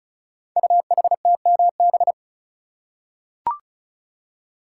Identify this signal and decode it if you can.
Morse code, heard as a keyed tone, about 36 wpm, 700 Hz. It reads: UHTMB
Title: Code Groups - 5 Letter:Numbers - 35wpm